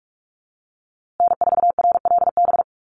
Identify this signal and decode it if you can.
Morse code, heard as a keyed tone, about 44 wpm, 700 Hz. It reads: N4RLB
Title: USA Call Signs Repeated 2x 45wpm